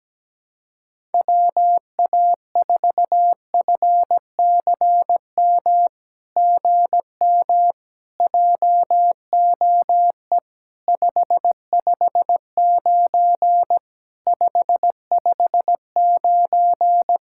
Morse code, heard as a keyed tone, about 17 wpm, 700 Hz. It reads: WA4FCM GM JOE 559 559
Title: SOTA QSO Practice 17wpm